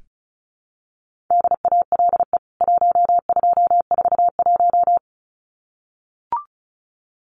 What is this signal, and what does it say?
Morse code, heard as a keyed tone, about 34 wpm, 700 Hz. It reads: DALE 1241
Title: CWT Exchange 35wpm